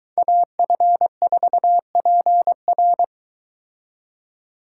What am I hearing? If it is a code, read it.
Morse code, heard as a keyed tone, about 22 wpm, 700 Hz. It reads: AF4PR